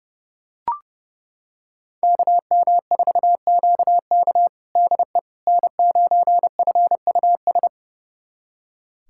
Morse code, heard as a keyed tone, about 30 wpm, 700 Hz. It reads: KM4QK DE N9FUS
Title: USA Callsign DE Callsign - 5 Characters - 30wpm